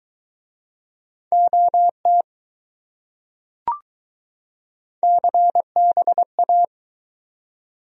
Morse code, heard as a keyed tone, about 21 wpm, 700 Hz. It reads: OT CBA